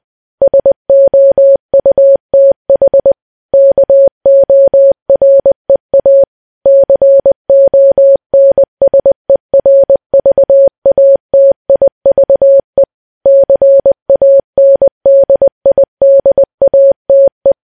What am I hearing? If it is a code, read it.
Morse code, heard as a keyed tone, about 20 wpm, 550 Hz. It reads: SOUTH KOREA CONSERVATIVE CANDIDATE
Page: news-headlines-in-morse-code-at-20-wpm